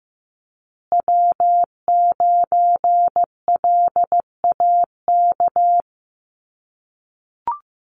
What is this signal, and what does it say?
Morse code, heard as a keyed tone, about 15 wpm, 700 Hz. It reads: W9LAK
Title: USA Call Signs Repeated 3x 15wpm